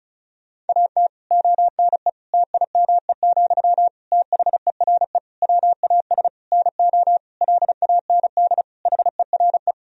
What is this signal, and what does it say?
Morse code, heard as a keyed tone, about 36 wpm, 700 Hz. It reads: AT ONE TIME, THERE WAS NO LAND HERE
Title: Sentences from Top 1000 Words - Review Part 1 - 35wpm